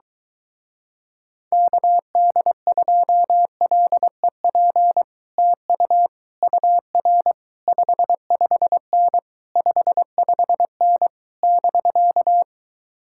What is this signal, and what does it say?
Morse code, heard as a keyed tone, about 22 wpm, 700 Hz. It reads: KD2LEP TU UR 55N 55N <BK>